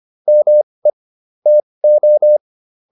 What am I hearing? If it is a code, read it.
Morse code, heard as a keyed tone, about 25 wpm, spaced as for 20 wpm, 600 Hz. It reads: ME TO